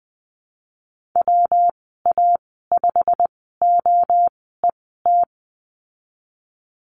Morse code, heard as a keyed tone, about 20 wpm, spaced as for 15 wpm, 700 Hz. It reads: WA5OET
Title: 167 Lesson 27g - Number 7 - Call Signs 20wpm